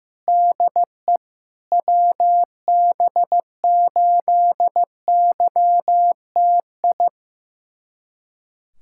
Morse code, heard as a keyed tone, about 15 wpm, 700 Hz. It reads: DE WB8YTI
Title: USA Callsign DE Callsign - 6 Characters - 15wpm